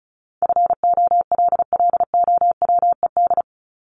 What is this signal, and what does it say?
Morse code, heard as a keyed tone, about 34 wpm, 700 Hz. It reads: FOLLOWED
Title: ED Suffix - Top 100 Words with Base Words 35wpm